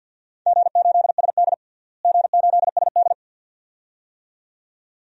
Morse code, heard as a keyed tone, about 46 wpm, 700 Hz. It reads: G8SD G8SD
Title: DXCC-100 Call Signs - 4 Characters - Repeated 2x - 50wpm